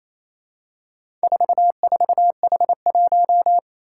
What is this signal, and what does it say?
Morse code, heard as a keyed tone, about 28 wpm, 700 Hz. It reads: 44H1
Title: Four Challenging Letter-Number Combinations 28wpm